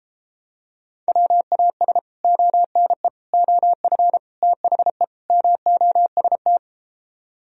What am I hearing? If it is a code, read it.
Morse code, heard as a keyed tone, about 33 wpm, 700 Hz. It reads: WAS ONE OF THE MOST